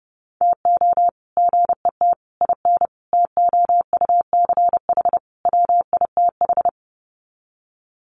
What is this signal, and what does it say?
Morse code, heard as a keyed tone, about 30 wpm, 700 Hz. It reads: TO GET IN TOUCH WITH
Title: Sets of 5 Words Encore2 30wpm